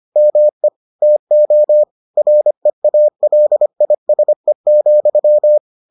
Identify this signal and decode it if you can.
Morse code, heard as a keyed tone, about 25 wpm, 600 Hz. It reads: ME TO REALISE,